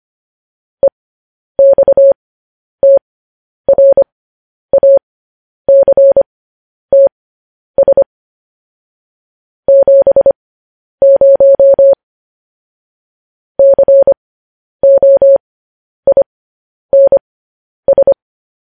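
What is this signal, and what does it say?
Morse code, heard as a keyed tone, about 25 wpm, spaced as for 10 wpm, 550 Hz. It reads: EXTRACTS 70 COINS
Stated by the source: Morse code news headlines sent at 10.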